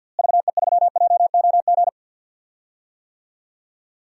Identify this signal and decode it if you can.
Morse code, heard as a keyed tone, about 52 wpm, 700 Hz. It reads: VE3JOG
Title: DXCC-100 Call Signs - 6 Characters - Repeated 3x - 50wpm